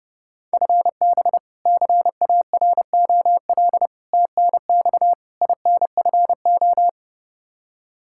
Morse code, heard as a keyed tone, about 30 wpm, 700 Hz. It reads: FB CAROL TNX INFO
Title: K1USN SST QSO Practice 30wpm